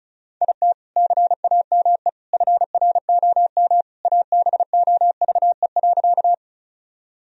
Morse code, heard as a keyed tone, about 34 wpm, 700 Hz. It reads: IT CAME FROM ABOVE.
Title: Sentences from Top 1000 Words - Review Part 2 - 35wpm